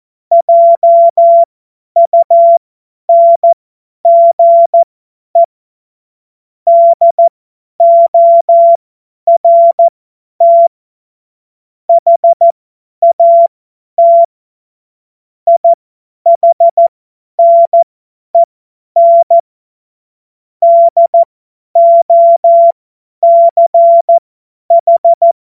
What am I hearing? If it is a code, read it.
Morse code, heard as a keyed tone, about 14 wpm, spaced as for 10 wpm, 680 Hz. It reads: JUNGE DORT HAT IHNEN DOCH
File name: Morsekurs / Witze / Witz 15. November 2022 50 F70